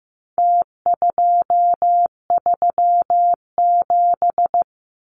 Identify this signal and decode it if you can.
Morse code, heard as a keyed tone, about 15 wpm, 700 Hz. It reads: T237
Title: Numbers - Contest Style - 4-Digit 15wpm